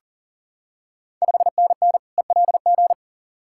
Morse code, heard as a keyed tone, about 39 wpm, 690 Hz. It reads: 5NN ELG
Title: Ontario QSO Party Exchange 40wpm